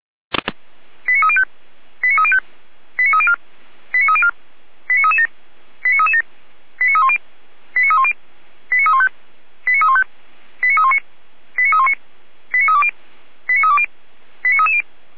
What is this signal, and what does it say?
Five-tone selective call (ZVEI), heard as quick runs of five tones, tones of about 70 ms. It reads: 89305 89305 89304 89304 89337 89337 89310 89310 89315 89315 89319 89319 89320 89320 89330